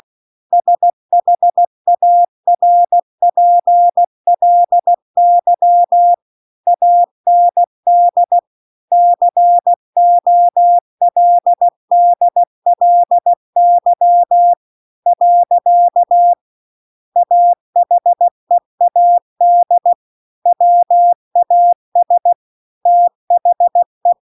Morse code, heard as a keyed tone, about 16 wpm, 700 Hz. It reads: SHARPLY AND COLDLY . AHEAD WAS THE